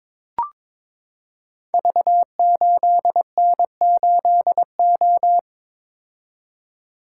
Morse code, heard as a keyed tone, about 22 wpm, 700 Hz. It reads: V8N8O
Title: Code Groups - 5 Letter:Numbers - 22wpm